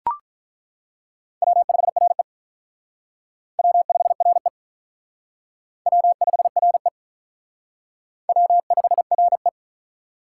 Morse code, heard as a keyed tone, about 43 wpm, 700 Hz. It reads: W5RE W5RE W5RE W5RE
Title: Speed Racing - USA 1x2 Call Signs 35wpm